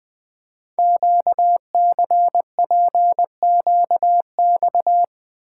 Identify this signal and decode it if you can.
Morse code, heard as a keyed tone, about 20 wpm, 700 Hz. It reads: QCPQX
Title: Code Groups - 5 Letters - 20wpm